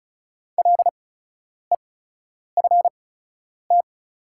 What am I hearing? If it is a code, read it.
Morse code, heard as a keyed tone, about 36 wpm, spaced as for 10 wpm, 700 Hz. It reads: LEFT
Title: 8x Character Spacing - Top 300 Words - 35wpm